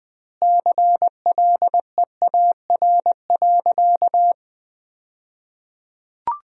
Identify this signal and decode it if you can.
Morse code, heard as a keyed tone, about 20 wpm, 700 Hz. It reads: CLEAR.
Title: 2x Word-Spacing - Sentences from Top 500 Words - 20wpm